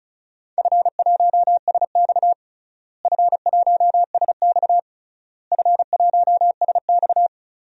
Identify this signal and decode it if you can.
Morse code, heard as a keyed tone, about 36 wpm, 700 Hz. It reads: F1SX F1SX F1SX
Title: DXCC-100 Call Signs - 4 Characters - Repeated 3x - 35wpm